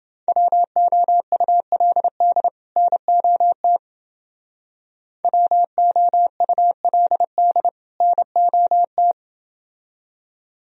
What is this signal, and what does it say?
Morse code, heard as a keyed tone, about 28 wpm, 700 Hz. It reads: WOULD NOT WOULD NOT